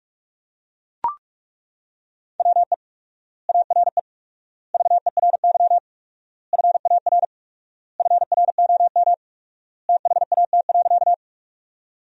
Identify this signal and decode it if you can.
Morse code, heard as a keyed tone, about 42 wpm, 700 Hz. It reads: WE ARE VERY FAR FROM THAT.
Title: 4x Word-Spacing - Sentences from Top 300 Words - 45wpm